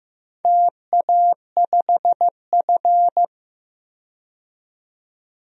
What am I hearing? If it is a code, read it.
Morse code, heard as a keyed tone, about 15 wpm, 700 Hz. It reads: TA5F
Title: DXCC-100 Call Signs - 4 Characters - 15wpm